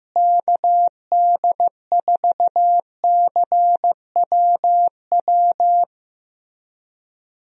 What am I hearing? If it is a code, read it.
Morse code, heard as a keyed tone, about 15 wpm, 700 Hz. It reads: KD4CWW
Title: USA Call Signs - 6 Characters - Repeated 3x - 15wpm